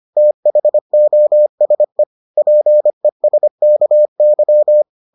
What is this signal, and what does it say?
Morse code, heard as a keyed tone, about 25 wpm, 600 Hz. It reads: THOSE PESKY